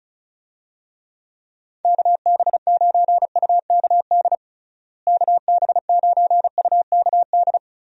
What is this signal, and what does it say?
Morse code, heard as a keyed tone, about 36 wpm, 700 Hz. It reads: KB9UKD KB9UKD
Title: USA Call Signs - 6 Characters - Repeated 3x - 35wpm